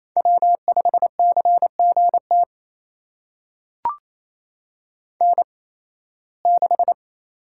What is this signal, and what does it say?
Morse code, heard as a keyed tone, about 28 wpm, spaced as for 18 wpm, 700 Hz. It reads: W5CGT N6
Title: USA Call Signs - 5 Characters - 8x Character Spacing - 28wpm